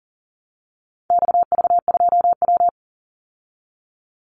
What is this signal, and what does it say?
Morse code, heard as a keyed tone, about 40 wpm, 700 Hz. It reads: XV2W
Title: Call Signs Repeated 3x - 40wpm